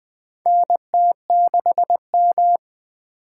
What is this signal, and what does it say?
Morse code, heard as a keyed tone, about 20 wpm, 700 Hz. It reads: NT6M
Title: USA Call Signs - 3x Character Spacing - 20wpm